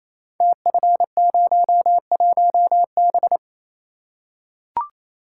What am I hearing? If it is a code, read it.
Morse code, heard as a keyed tone, about 28 wpm, 700 Hz. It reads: TF01B
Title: Code Groups - 5 Letter:Numbers - 28wpm